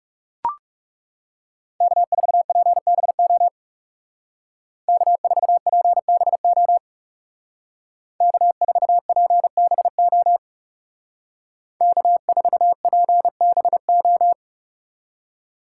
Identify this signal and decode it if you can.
Morse code, heard as a keyed tone, about 36 wpm, 700 Hz. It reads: K4PBO K4PBO K4PBO K4PBO
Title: Speed Racing - USA 1x3 Call Signs 30wpm